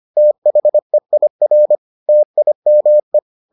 Morse code, heard as a keyed tone, about 25 wpm, 600 Hz. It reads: THEIR TIME